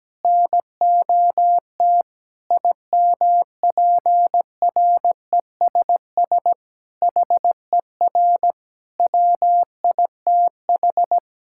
Morse code, heard as a keyed tone, about 17 wpm, 700 Hz. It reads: NOT IMPRESS HER WITH
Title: Sentences from Top 1600 Words 17wpm